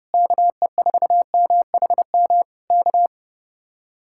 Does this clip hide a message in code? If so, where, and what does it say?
Yes: Morse code from 0.1 to 3.1 s, heard as a keyed tone, about 28 wpm, 700 Hz. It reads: KE4MHM K